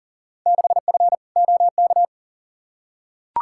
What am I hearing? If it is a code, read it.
Morse code, heard as a keyed tone, about 40 wpm, 700 Hz. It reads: 6F OK
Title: Field Day Exchanges Encore - 40wpm